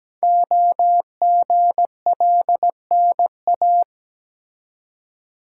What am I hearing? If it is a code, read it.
Morse code, heard as a keyed tone, about 17 wpm, 700 Hz. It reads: OGLNA
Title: Code Groups - 5 Letters - 17wpm